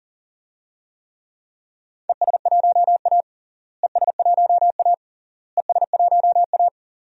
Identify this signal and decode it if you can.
Morse code, heard as a keyed tone, about 40 wpm, 700 Hz. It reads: ES1A ES1A ES1A
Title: DXCC-100 Call Signs - 4 Characters - Repeated 3x - 40wpm